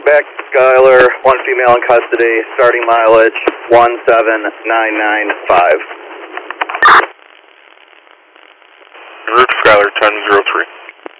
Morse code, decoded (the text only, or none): none